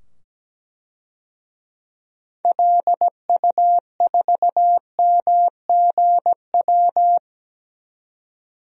Morse code, heard as a keyed tone, about 17 wpm, 700 Hz. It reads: LU4MGW
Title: DXCC-100 Call Signs - 6 Characters - 17wpm